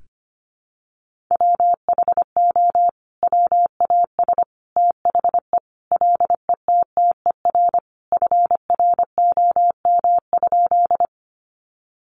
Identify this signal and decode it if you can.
Morse code, heard as a keyed tone, about 25 wpm, 700 Hz. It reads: WHO WAS THE LETTER FROM?
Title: Sentences from Top 300 Words 25wpm